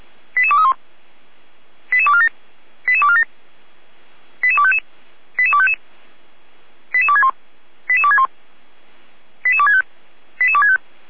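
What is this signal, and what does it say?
Five-tone selective call (ZVEI), heard as quick runs of five tones, tones of about 70 ms. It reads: 88231 88247 88247 88255 88255 88261 88261 88265 88265
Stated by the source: ZVEI five-tone selective call received by SDR.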